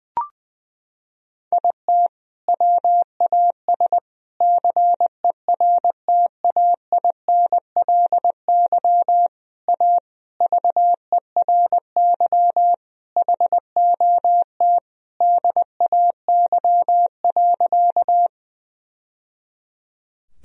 Morse code, heard as a keyed tone, about 21 wpm, 700 Hz. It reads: IT WAS CERTAINLY A VERY HOT DAY.
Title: Sentences from Top 1000 Words - Review Part 3 - 20wpm